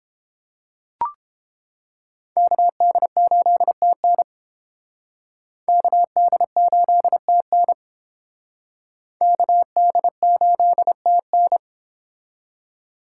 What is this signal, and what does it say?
Morse code, heard as a keyed tone, about 30 wpm, 700 Hz. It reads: KD8TN KD8TN KD8TN